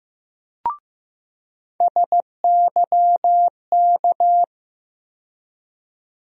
Morse code, heard as a keyed tone, about 15 wpm, 700 Hz. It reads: SYK
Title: USA Call Sign Postfixes - 3 Letters 15wpm